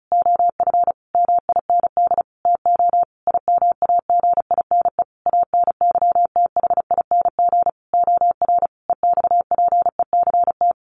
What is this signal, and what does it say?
Morse code, heard as a keyed tone, about 34 wpm, 700 Hz. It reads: OF MIND TO IMAGINE ANYTHING OR EXPECT
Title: Sentences from Top 1400 Words 35wpm